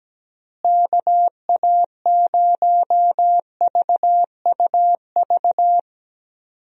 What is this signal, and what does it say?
Morse code, heard as a keyed tone, about 17 wpm, 700 Hz. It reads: KA0VUV
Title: USA Call Signs - 6 Characters - Repeated 2x - 17wpm